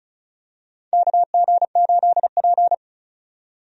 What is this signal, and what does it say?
Morse code, heard as a keyed tone, about 36 wpm, 700 Hz. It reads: KG8P
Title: USA Call Signs - 4 Characters - 35wpm